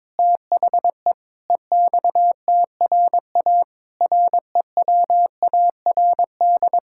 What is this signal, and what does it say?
Morse code, heard as a keyed tone, about 22 wpm, 700 Hz. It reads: THE EXTRA REWARD
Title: Pangram Challenge v2 - 22wpm